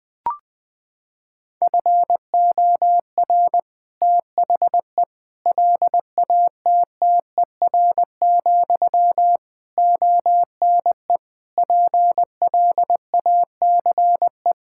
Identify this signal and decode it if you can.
Morse code, heard as a keyed tone, about 21 wpm, 700 Hz. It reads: FOR THE LATTER, ONE PLACE
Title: Sentences from Top 1600 Words 20wpm